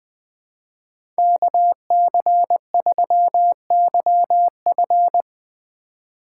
Morse code, heard as a keyed tone, about 20 wpm, 700 Hz. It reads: KC3YF
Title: USA Call Signs - 5 Characters - 4x Character Spacing - 20wpm